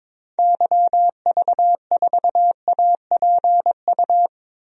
Morse code, heard as a keyed tone, about 22 wpm, 700 Hz. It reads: YV4APU
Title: DXCC-100 Call Signs - 6 Characters - 22wpm